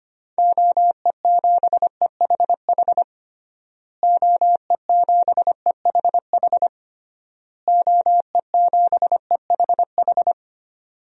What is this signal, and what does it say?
Morse code, heard as a keyed tone, about 25 wpm, 700 Hz. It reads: OE7EHH OE7EHH OE7EHH
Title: DXCC-100 Call Signs - 6 Characters - Repeated 3x - 25wpm